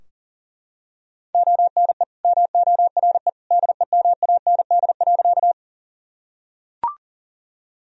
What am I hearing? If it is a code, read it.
Morse code, heard as a keyed tone, about 37 wpm, 700 Hz. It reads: ONE MORE DEMAND.
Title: Sentences from Top 1000 Words - Review Part 4 - 40wpm